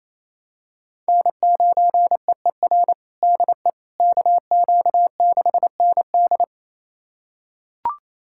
Morse code, heard as a keyed tone, about 27 wpm, 700 Hz. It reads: N9EER DE KQ6ND
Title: USA Callsign DE Callsign - 5 Characters - 28wpm